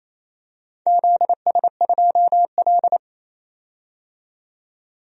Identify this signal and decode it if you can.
Morse code, heard as a keyed tone, about 28 wpm, 700 Hz. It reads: ZS2L
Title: DXCC-100 Call Signs - 4 Characters - 28wpm